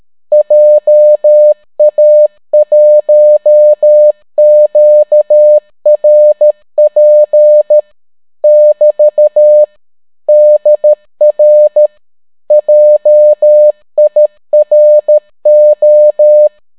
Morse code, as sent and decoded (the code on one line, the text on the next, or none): .--- .- .---- --.- .-. .--. / -...- / -.. .-. / .--- .. .-. ---
JA1QRP <BT> DR JIRO